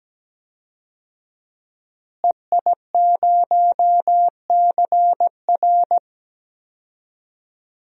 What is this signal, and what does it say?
Morse code, heard as a keyed tone, about 17 wpm, 700 Hz. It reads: EI0CR
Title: DXCC-100 Call Signs - 5 Characters - Repeated 2x - 17wpm